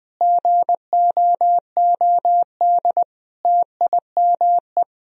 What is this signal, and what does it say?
Morse code, heard as a keyed tone, about 20 wpm, 700 Hz. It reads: GOOD TIME
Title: Sentences from Top 1000 Words - Review Part 2 - 20wpm